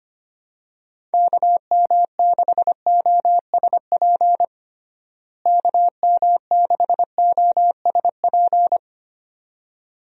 Morse code, heard as a keyed tone, about 25 wpm, 700 Hz. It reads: KM6OSP KM6OSP
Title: USA Call Signs - 6 Characters - Repeated 2x - 25wpm